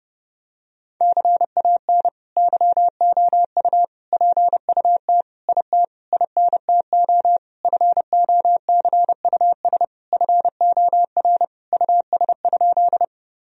Morse code, heard as a keyed tone, about 30 wpm, 700 Hz. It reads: CAN YOU PUT IT INTO FOCUS FOR US?
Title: Sentences from Top 1000 Words - Review Part 4 - 30wpm